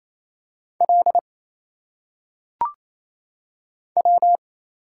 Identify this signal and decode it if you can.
Morse code, heard as a keyed tone, about 27 wpm, 700 Hz. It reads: L W
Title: Single Letters - Phonetic Pronunciation 28wpm